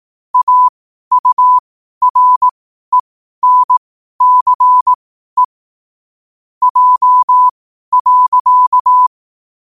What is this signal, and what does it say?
Morse code, heard as a keyed tone, about 18 wpm, spaced as for 12 wpm, 1000 Hz. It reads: AURENCE J.
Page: 12 wpm morse code quotes